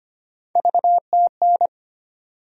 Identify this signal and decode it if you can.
Morse code, heard as a keyed tone, about 25 wpm, 700 Hz. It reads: VTN
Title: USA Call Sign Postfixes - 3 Letters 25wpm